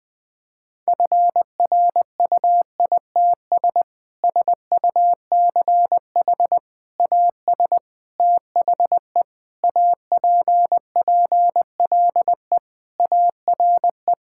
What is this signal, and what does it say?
Morse code, heard as a keyed tone, about 21 wpm, 700 Hz. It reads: FRUITS SUCH AS THE APPLE ARE
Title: Sentences with 3-Word Phrases - Spoken Beginning - 20wpm